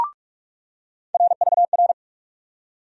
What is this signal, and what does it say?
Morse code, heard as a keyed tone, about 43 wpm, 700 Hz. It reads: RVR
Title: USA Call Sign Postfixes - 3 Letters 45wpm